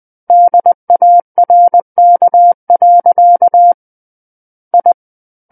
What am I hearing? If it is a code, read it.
Morse code, heard as a keyed tone, about 20 wpm, 700 Hz. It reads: DARK. I
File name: 20 WPM